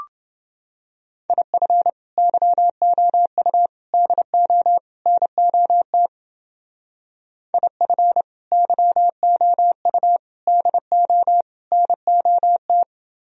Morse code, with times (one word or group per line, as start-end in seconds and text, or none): 1.3-1.9 IF
2.2-3.7 YOU
3.9-4.8 DO
5.1-6.1 NOT
7.5-8.2 IF
8.5-10.2 YOU
10.5-11.4 DO
11.7-12.8 NOT